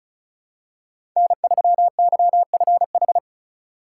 Morse code, heard as a keyed tone, about 36 wpm, 700 Hz. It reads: N3YFH
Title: USA Call Signs - 5 Characters - 2x Character Spacing - 35wpm